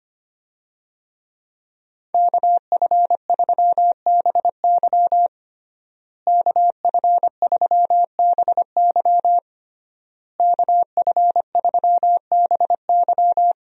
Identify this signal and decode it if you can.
Morse code, heard as a keyed tone, about 25 wpm, 700 Hz. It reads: KF3BY KF3BY KF3BY
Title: USA Call Signs - 5 Characters - Repeated 3x - 25wpm